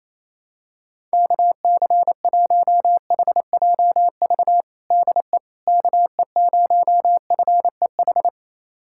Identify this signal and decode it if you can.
Morse code, heard as a keyed tone, about 28 wpm, 700 Hz. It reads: KC1HJV DE KE0FEH
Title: USA Callsign DE Callsign 28wpm